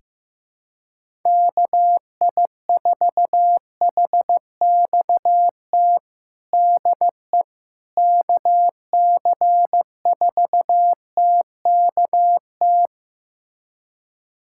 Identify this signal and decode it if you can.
Morse code, heard as a keyed tone, about 15 wpm, 700 Hz. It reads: KI4HXT DE KC4TKT
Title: USA Callsign DE Callsign - 6 Characters - 15wpm